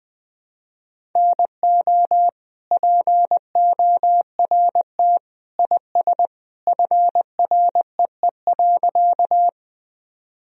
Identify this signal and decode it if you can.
Morse code, heard as a keyed tone, about 20 wpm, 700 Hz. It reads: NO PORT IS FREE.
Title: Sentences from Top 1000 Words - Review Part 1 - 20wpm